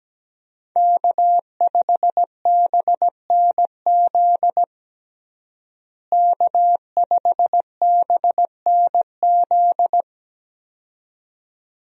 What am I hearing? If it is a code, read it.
Morse code, heard as a keyed tone, about 17 wpm, 700 Hz. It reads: K5BNZ K5BNZ
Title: USA Call Signs - 5 Characters - Repeated 2x - 17wpm